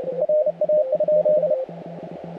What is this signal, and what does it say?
Morse code, heard as a keyed tone, about 31 wpm, 610 Hz, with radio noise. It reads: RA?
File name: audio mp3 files / A71BX 40m CW 2008-4-12 2-0-53